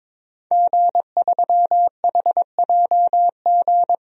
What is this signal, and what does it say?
Morse code, heard as a keyed tone, about 22 wpm, 700 Hz. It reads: G3HJG
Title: DXCC-100 Call Signs - 5 Characters - 22wpm